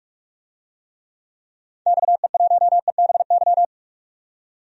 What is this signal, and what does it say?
Morse code, heard as a keyed tone, about 47 wpm, 700 Hz. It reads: XE1EBY
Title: DXCC-100 Call Signs - 6 Characters - 45wpm